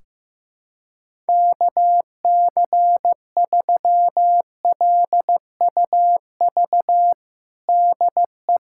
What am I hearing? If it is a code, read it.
Morse code, heard as a keyed tone, about 15 wpm, 700 Hz. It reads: KC3LUV DE
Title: USA Callsign DE Callsign 15wpm